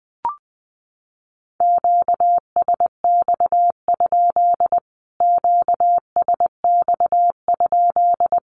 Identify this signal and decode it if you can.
Morse code, heard as a keyed tone, about 20 wpm, 700 Hz. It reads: QSX? QSX?
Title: Q-Codes Repeated 3x 20wpm